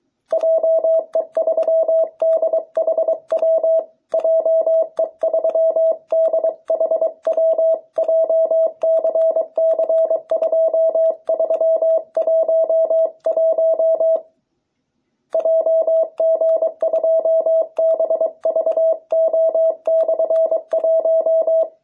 Morse code, heard as a keyed tone, about 23 wpm, 660 Hz. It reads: JE3DHW JE3DHW JCC2311 JG2BVO/1